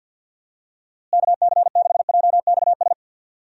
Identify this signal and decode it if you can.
Morse code, heard as a keyed tone, about 47 wpm, 700 Hz. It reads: KC6JXS
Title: USA Call Signs - 6 Characters - Repeated 2x - 50wpm